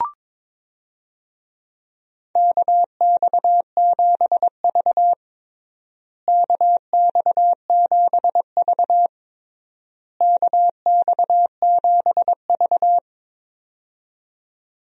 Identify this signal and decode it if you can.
Morse code, heard as a keyed tone, about 22 wpm, 700 Hz. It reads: KX7V KX7V KX7V
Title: USA Call Signs - 4 Characters - Repeated 3x - 22wpm